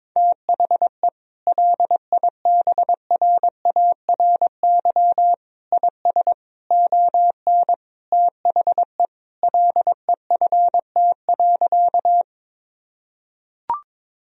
Morse code, heard as a keyed tone, about 21 wpm, 700 Hz. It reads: THE LIBRARY IS ON THE LEFT.
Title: Sentences from Top 1000 Words - Review Part 4 - 22wpm